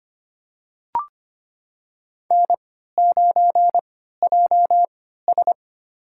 Morse code, heard as a keyed tone, about 25 wpm, spaced as for 14 wpm, 700 Hz. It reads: N9JS
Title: USA Call Signs - 4 Characters - 3x Character Spacing - 25wpm